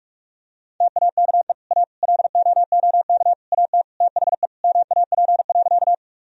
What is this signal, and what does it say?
Morse code, heard as a keyed tone, about 43 wpm, 700 Hz. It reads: TAKE A LOOK AT THE MAP.
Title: Sentences from Top 1000 Words - Review Part 2 - 45wpm